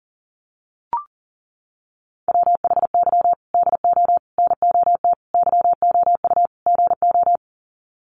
Morse code, heard as a keyed tone, about 40 wpm, 700 Hz. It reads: WHY DO NOT YOU GO
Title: Sets of 5 Words Encore 40wpm